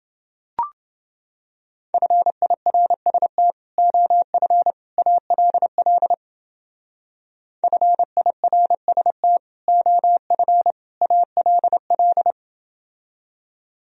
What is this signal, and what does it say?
Morse code, heard as a keyed tone, about 29 wpm, 700 Hz. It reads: FIRST OF ALL FIRST OF ALL